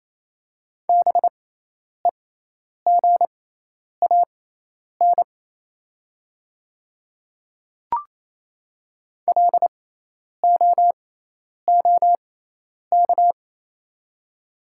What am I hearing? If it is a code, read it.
Morse code, heard as a keyed tone, about 27 wpm, spaced as for 10 wpm, 700 Hz. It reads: BEGAN LOOK
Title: 6x Character Spacing - Top 300 Words - 28wpm